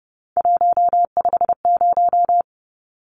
Morse code, heard as a keyed tone, about 29 wpm, 700 Hz. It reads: 150
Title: Numbers - 3-Digit 30wpm